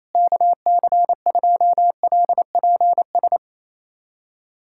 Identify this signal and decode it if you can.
Morse code, heard as a keyed tone, about 28 wpm, 700 Hz. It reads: KC2LPS
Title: USA Callsign DE Callsign 28wpm